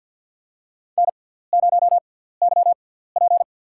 Morse code, heard as a keyed tone, about 50 wpm, spaced as for 17 wpm, 700 Hz. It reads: N0YP
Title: USA Call Signs - 4 Characters - 6x Character Spacing - 50wpm